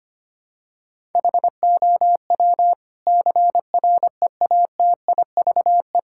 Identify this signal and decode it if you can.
Morse code, heard as a keyed tone, about 25 wpm, 700 Hz. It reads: HOW CREATIVE
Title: Sentences from Top 1000 Words - Review Part 5 - 25wpm